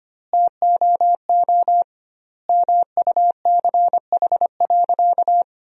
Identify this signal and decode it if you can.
Morse code, heard as a keyed tone, about 25 wpm, 700 Hz. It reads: TOO MUCH.
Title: 2x Word-Spacing - Sentences from Top 300 Words - 25wpm